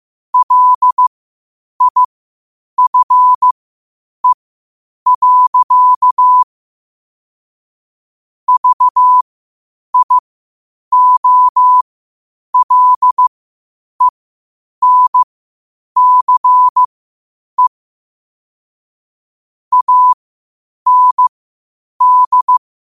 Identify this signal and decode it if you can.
Morse code, heard as a keyed tone, about 15 wpm, spaced as for 8 wpm, 1000 Hz. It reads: LIFE. VIOLENCE AND
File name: qotdfiles / 8wpm